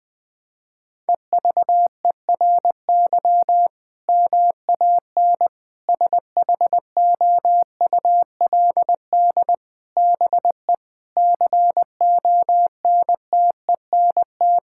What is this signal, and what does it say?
Morse code, heard as a keyed tone, about 21 wpm, 700 Hz. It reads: EVERY MAN SHOULD BE CONTENT
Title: Sentences from Top 1000 Words - Review Part 6 - 20wpm